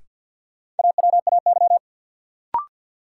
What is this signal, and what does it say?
Morse code, heard as a keyed tone, about 53 wpm, 700 Hz. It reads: AWAY
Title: Sets of 2 Words - New Word Drill - 50wpm